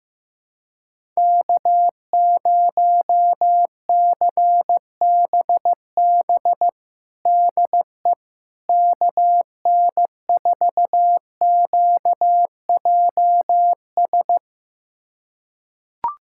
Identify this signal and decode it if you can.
Morse code, heard as a keyed tone, about 15 wpm, 700 Hz. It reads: K0CBB DE KN4QJS
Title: USA Callsign DE Callsign 15wpm